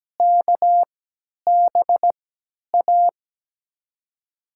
Morse code, heard as a keyed tone, about 17 wpm, spaced as for 10 wpm, 700 Hz. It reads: KBA
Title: USA Call Signs - 6 Characters - 3x Character Spacing - 17wpm